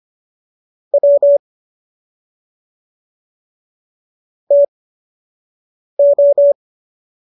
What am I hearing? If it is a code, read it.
Morse code, heard as a keyed tone, about 25 wpm, spaced as for 6 wpm, 570 Hz. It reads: W TO
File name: qso003 06